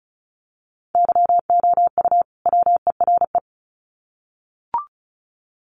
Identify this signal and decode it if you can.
Morse code, heard as a keyed tone, about 34 wpm, 700 Hz. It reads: YOU WERE
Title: Sets of 2 Words 35wpm